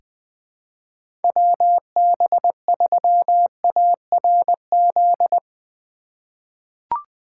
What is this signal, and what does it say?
Morse code, heard as a keyed tone, about 20 wpm, 700 Hz. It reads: WB3ARZ
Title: USA Call Signs - 6 Characters - 3x Character Spacing - 20wpm